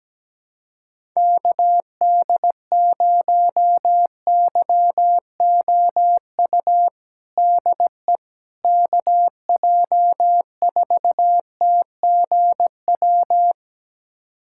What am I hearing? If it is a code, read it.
Morse code, heard as a keyed tone, about 17 wpm, 700 Hz. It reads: KD0YOU DE KJ4TGW
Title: USA Callsign DE Callsign - 6 Characters - 17wpm